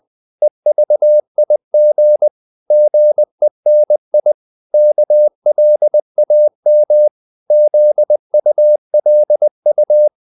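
Morse code, heard as a keyed tone, about 20 wpm, 600 Hz. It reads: EVIG GENI KLAM ZULU
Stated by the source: Morse code practice of short Danish words at 20 wpm, from CW med Gnister.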